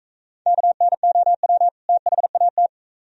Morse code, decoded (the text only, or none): KNOW THAT